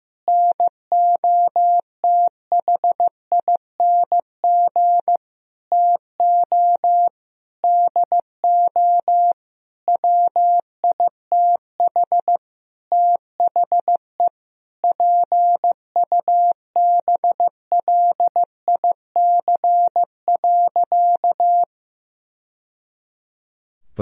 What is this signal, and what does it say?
Morse code, heard as a keyed tone, about 16 wpm, 700 Hz. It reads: NOTHING TO DO WITH THE PUBLIC.
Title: Sentences from Top 1000 Words - Review Part 3 - 15wpm